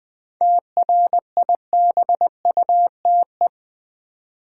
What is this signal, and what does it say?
Morse code, heard as a keyed tone, about 20 wpm, 700 Hz. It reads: TRIBUTE
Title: 7 Letter Words - First 3 Letters Given - 20wpm